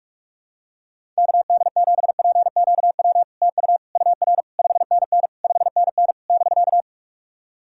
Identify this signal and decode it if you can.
Morse code, heard as a keyed tone, about 46 wpm, 690 Hz. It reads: KD7PQW TU UR 5NN 5NN <BK>
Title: SOTA QSO Practice 45wpm